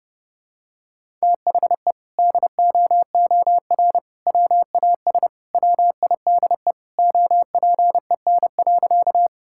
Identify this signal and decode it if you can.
Morse code, heard as a keyed tone, about 30 wpm, 700 Hz. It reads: THE DOOR WAS WIDE OPEN.
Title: Sentences from Top 1000 Words - Review Part 6 - 30wpm